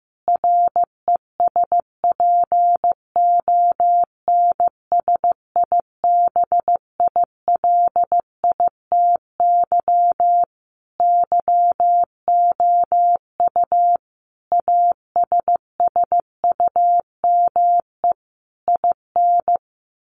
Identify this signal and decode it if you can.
Morse code, heard as a keyed tone, about 15 wpm, 700 Hz. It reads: RESPONSIBILITY YOU ASSUME IN
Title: Sentences from Top 1500 Words 15wpm